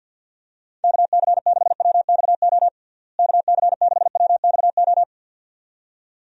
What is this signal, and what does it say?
Morse code, heard as a keyed tone, about 51 wpm, 700 Hz. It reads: KC6WXO KC6WXO
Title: USA Call Signs - 6 Characters - Repeated 2x - 50wpm